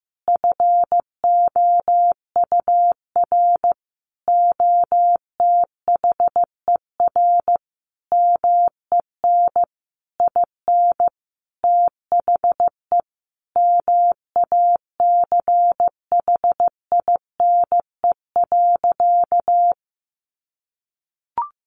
Morse code, heard as a keyed tone, about 15 wpm, 700 Hz. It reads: FOUR OTHER MEN IN THE MACHINE.
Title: A Sound Of Thunder 15wpm